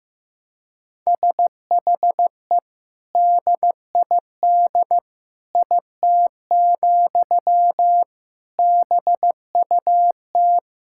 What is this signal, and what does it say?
Morse code, heard as a keyed tone, about 15 wpm, 700 Hz. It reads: SHE DID IT, BUT
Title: Sentences from Top 1000 Words - Review Part 6 - 15wpm